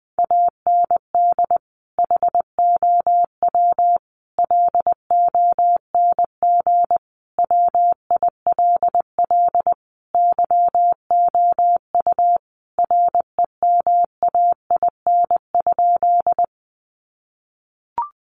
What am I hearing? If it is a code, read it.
Morse code, heard as a keyed tone, about 19 wpm, 700 Hz. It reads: AND HOW LONG WILL YOU REMAIN?
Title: Sentences from Top 1200 Words 20wpm